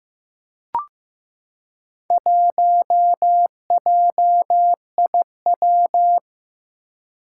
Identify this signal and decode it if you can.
Morse code, heard as a keyed tone, about 15 wpm, 700 Hz. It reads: 1JIW
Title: Four Challenging Letter-Number Combinations 15wpm